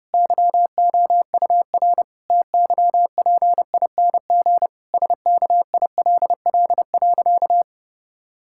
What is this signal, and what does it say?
Morse code, heard as a keyed tone, about 30 wpm, 700 Hz. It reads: YOUR TYPING SKILL.
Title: Pangram Challenge v2 - 30wpm